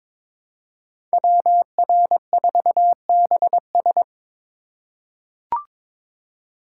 Morse code, heard as a keyed tone, about 22 wpm, 700 Hz. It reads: WR4BS
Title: USA Call Signs - 5 Characters - 3x Character Spacing - 22wpm